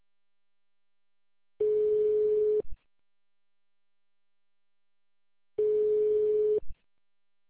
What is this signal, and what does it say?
Telephone call progress tone (425 Hz, 1 s on, 3 s off), heard as a telephone network's tone, about 420 Hz. Ringback tone.